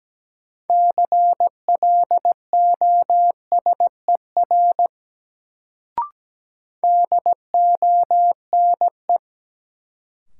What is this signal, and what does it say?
Morse code, heard as a keyed tone, about 17 wpm, 700 Hz. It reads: CLOSER DONE
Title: Sets of 5 Words Encore - New Word Drill - 17wpm